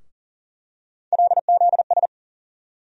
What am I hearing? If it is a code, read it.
Morse code, heard as a keyed tone, about 40 wpm, 700 Hz. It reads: LZS
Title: USA Call Sign Postfixes - 3 Letters 40wpm